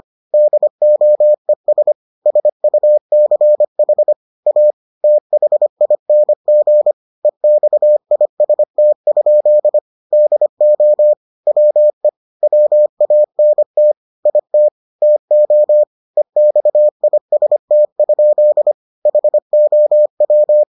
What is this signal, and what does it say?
Morse code, heard as a keyed tone, about 25 wpm, 600 Hz. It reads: DOES SUCH A THING EXIST? DO WE WANT IT TO EXIST? HOW